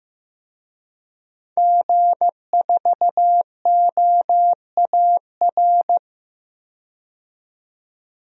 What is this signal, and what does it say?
Morse code, heard as a keyed tone, about 15 wpm, 700 Hz. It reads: G4OAR
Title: DXCC-100 Call Signs - 5 Characters - 15wpm